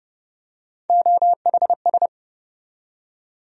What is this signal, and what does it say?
Morse code, heard as a keyed tone, about 30 wpm, 700 Hz. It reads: OHS
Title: USA Call Sign Postfixes - 3 Letters 30wpm